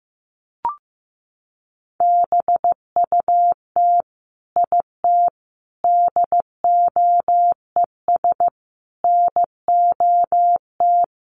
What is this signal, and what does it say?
Morse code, heard as a keyed tone, about 15 wpm, 700 Hz. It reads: BUT IT DOES NOT
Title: Sentences from Top 1200 Words 15wpm